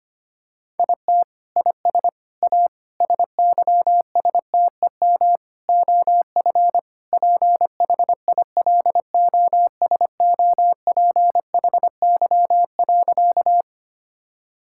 Morse code, heard as a keyed tone, about 25 wpm, 700 Hz. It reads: IT IS A SYSTEM OF PHILOSOPHY.
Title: Sentences from Top 1000 Words - Review Part 6 - 25wpm